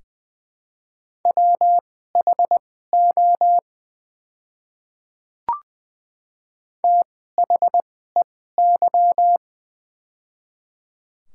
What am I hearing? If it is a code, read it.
Morse code, heard as a keyed tone, about 20 wpm, spaced as for 15 wpm, 700 Hz. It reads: WHO THEY
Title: 2x Character Spacing - Top 100 Words - 20wpm